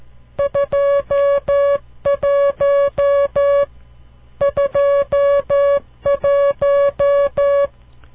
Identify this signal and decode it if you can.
Morse code, heard as a keyed tone, about 13 wpm, 560 Hz. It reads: 21 21
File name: CW Learning Programs / farnsworth / unit 5